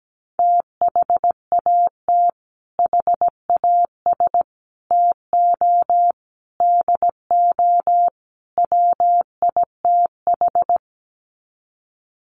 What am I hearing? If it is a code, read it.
Morse code, heard as a keyed tone, about 17 wpm, 700 Hz. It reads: THAT HAS TO DO WITH
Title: Sets of 5 Words Encore2 17wpm